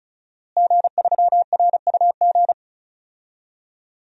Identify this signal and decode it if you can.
Morse code, heard as a keyed tone, about 36 wpm, 700 Hz. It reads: G3RUG
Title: DXCC-100 Call Signs - 5 Characters - Repeated 2x - 35wpm